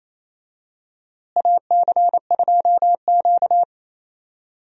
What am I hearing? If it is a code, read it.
Morse code, heard as a keyed tone, about 27 wpm, 700 Hz. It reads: AC2Q
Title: USA Call Signs - 4 Characters - Repeated 2x - 28wpm